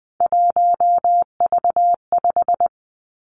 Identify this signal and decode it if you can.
Morse code, heard as a keyed tone, about 19 wpm, 700 Hz. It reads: 1V5
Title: Three Challenging Letter-Number Combinations 20wpm